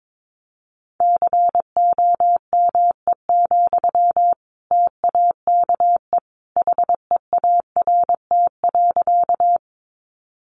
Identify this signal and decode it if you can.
Morse code, heard as a keyed tone, about 22 wpm, 700 Hz. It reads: COME, TAKE HEART.
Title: Sentences from Top 1200 Words 22wpm